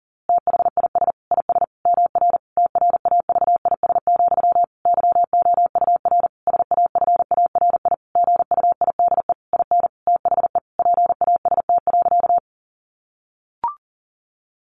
Morse code, heard as a keyed tone, about 40 wpm, 700 Hz. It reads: THIS IS MR TRAVIS, YOUR SAFARI GUIDE IN THE PAST.
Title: A Sound Of Thunder 40wpm